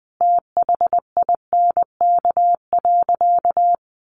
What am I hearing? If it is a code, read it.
Morse code, heard as a keyed tone, about 20 wpm, 700 Hz. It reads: THINK.
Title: Sentences from Top 600 Words 20wpm